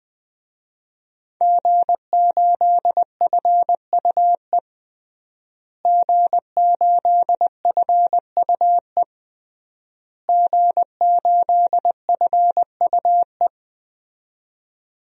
Morse code, heard as a keyed tone, about 20 wpm, 700 Hz. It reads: G8FUE G8FUE G8FUE
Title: DXCC-100 Call Signs - 5 Characters - Repeated 3x - 20wpm